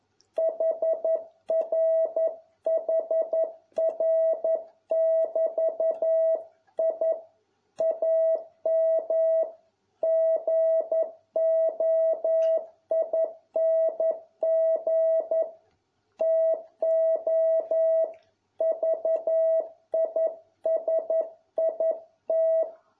Morse code, heard as a keyed tone, about 11 wpm, 650 Hz. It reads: HRHR<BT>I AM GOING TO VISIT